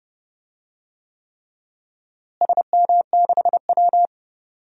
Morse code, heard as a keyed tone, about 30 wpm, 700 Hz. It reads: SM6W
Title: DXCC-100 Call Signs - 4 Characters - Repeated 3x - 30wpm